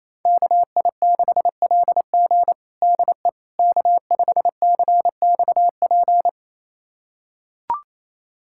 Morse code, heard as a keyed tone, about 27 wpm, 700 Hz. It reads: KI6LG DE K5CXP
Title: USA Callsign DE Callsign - 5 Characters - 28wpm